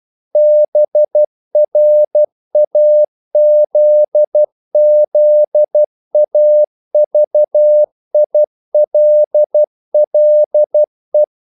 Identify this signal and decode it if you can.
Morse code, heard as a keyed tone, about 12 wpm, 600 Hz. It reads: BRAZZAVILLE
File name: CW / Practice / NationalCapitals12